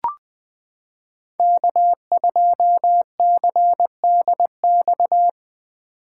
Morse code, heard as a keyed tone, about 20 wpm, 700 Hz. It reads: K2CDX
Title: K1USN SST QSO Practice - 20wpm